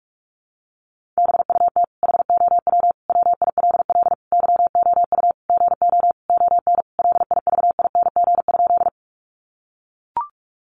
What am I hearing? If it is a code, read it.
Morse code, heard as a keyed tone, about 44 wpm, 700 Hz. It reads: BUT HOW WILL YOU GO ON LIVING?
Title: Sentences from Top 1100 Words 45wpm